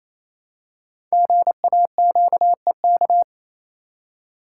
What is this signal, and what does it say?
Morse code, heard as a keyed tone, about 27 wpm, 700 Hz. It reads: GAQEK
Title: Code Groups - 5 Letters - 28wpm